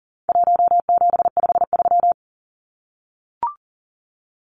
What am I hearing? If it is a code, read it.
Morse code, heard as a keyed tone, about 40 wpm, 700 Hz. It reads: 1753
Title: Numbers - 4-Digit 40wpm